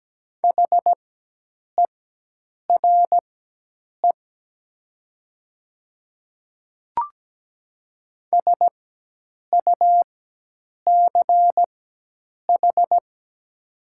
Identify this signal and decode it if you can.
Morse code, heard as a keyed tone, about 17 wpm, spaced as for 8 wpm, 700 Hz. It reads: HERE SUCH
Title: 4x Character Spacing - Top 500 Words - 17wpm